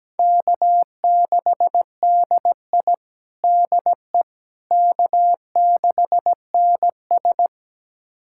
Morse code, heard as a keyed tone, about 17 wpm, 700 Hz. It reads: K6DI DE K6NS
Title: USA Callsign DE Callsign - 4 Characters - 17wpm